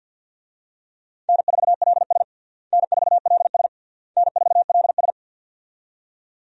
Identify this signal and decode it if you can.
Morse code, heard as a keyed tone, about 52 wpm, 700 Hz. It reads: N4LS N4LS N4LS
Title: USA Call Signs - 4 Characters - Repeated 3x - 50wpm